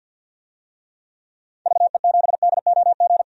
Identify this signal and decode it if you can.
Morse code, heard as a keyed tone, about 47 wpm, 700 Hz. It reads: VE7DOG
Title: DXCC-100 Call Signs - 6 Characters - 50wpm